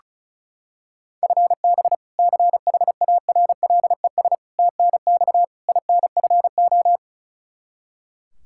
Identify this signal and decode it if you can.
Morse code, heard as a keyed tone, about 36 wpm, 700 Hz. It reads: FB CHARLES TNX INFO
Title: Practice QSO - Exchange 3 - More Info - 35wpm